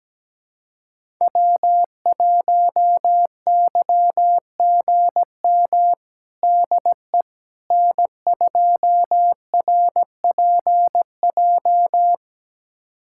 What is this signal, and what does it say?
Morse code, heard as a keyed tone, about 17 wpm, 700 Hz. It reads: W1YGM DE N2RPJ
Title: USA Callsign DE Callsign - 5 Characters - 17wpm